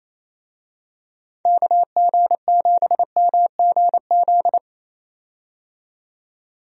Morse code, heard as a keyed tone, about 27 wpm, 700 Hz. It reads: KG7MGZ
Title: USA Call Signs - 6 Characters - 28wpm